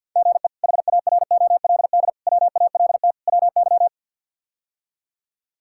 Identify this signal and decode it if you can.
Morse code, heard as a keyed tone, about 53 wpm, 690 Hz. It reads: GE HAROLD WALT WY
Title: K1USN SST QSO Practice - 50wpm